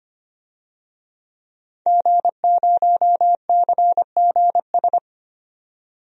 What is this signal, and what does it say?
Morse code, heard as a keyed tone, about 25 wpm, 700 Hz. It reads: G0CGS
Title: DXCC-100 Call Signs - 5 Characters - 25wpm